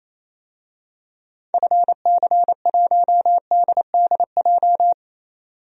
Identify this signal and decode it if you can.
Morse code, heard as a keyed tone, about 28 wpm, 700 Hz. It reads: FC1DDJ
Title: DXCC-100 Call Signs - 6 Characters - 28wpm